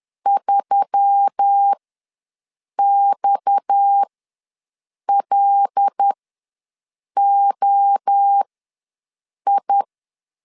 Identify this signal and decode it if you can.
Morse code, heard as a keyed tone, about 10 wpm, spaced as for 6 wpm, 800 Hz. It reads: 3XLOI
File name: LEZIONI / 8 YJ45 / 8 YJ4 ws 10wpm 4020